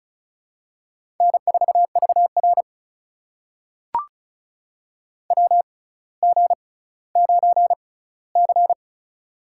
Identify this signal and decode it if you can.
Morse code, heard as a keyed tone, about 36 wpm, spaced as for 18 wpm, 700 Hz. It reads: N4VR WG9C
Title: USA Call Signs - 4 Characters - 6x Character Spacing - 35wpm